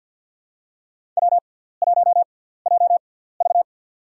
Morse code, heard as a keyed tone, about 52 wpm, spaced as for 17 wpm, 700 Hz. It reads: W1JV
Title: USA Call Signs - 4 Characters - 6x Character Spacing - 50wpm